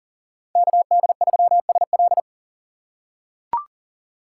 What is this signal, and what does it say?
Morse code, heard as a keyed tone, about 40 wpm, 700 Hz. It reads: KD3SL
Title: USA Call Signs - 3x Character Spacing - 40wpm